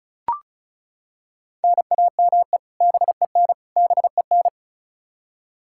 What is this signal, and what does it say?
Morse code, heard as a keyed tone, about 36 wpm, 700 Hz. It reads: NAME BEN BEN
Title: Practice QSO - Exchange 1 - Beginning - 35wpm